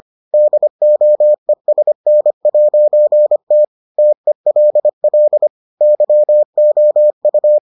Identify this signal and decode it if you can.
Morse code, heard as a keyed tone, about 25 wpm, 600 Hz. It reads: DOESN'T TELL YOU